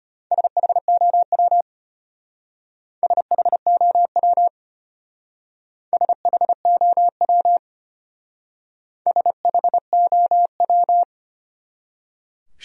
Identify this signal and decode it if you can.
Morse code, heard as a keyed tone, about 32 wpm, 700 Hz. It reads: SHOW SHOW SHOW SHOW